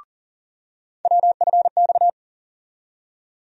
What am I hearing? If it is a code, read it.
Morse code, heard as a keyed tone, about 40 wpm, 700 Hz. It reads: WFX
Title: USA Call Sign Postfixes - 3 Letters 40wpm